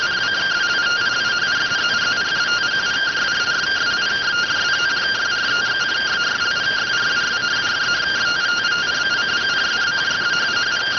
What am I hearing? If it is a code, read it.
Two-tone teleprinter signal, about 170 Hz shift.